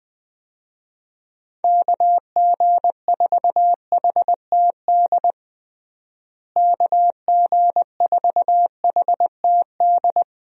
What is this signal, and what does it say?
Morse code, heard as a keyed tone, about 20 wpm, 700 Hz. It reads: KG4HTD KG4HTD
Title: USA Call Signs - 6 Characters - Repeated 2x - 20wpm